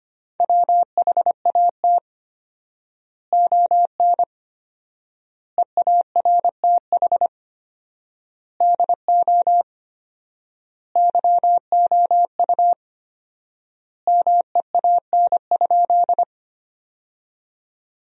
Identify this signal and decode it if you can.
Morse code, heard as a keyed tone, about 25 wpm, 700 Hz. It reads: WHAT ON EARTH DO YOU MEAN?
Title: 4x Word-Spacing - Sentences from Top 200 Words - 25wpm